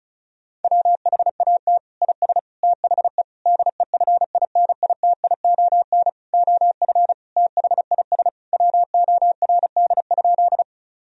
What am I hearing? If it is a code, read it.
Morse code, heard as a keyed tone, about 36 wpm, 700 Hz. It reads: WHAT IS THE DEFINITION OF THIS WORD?
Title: Sentences from Top 1000 Words - Review Part 4 - 35wpm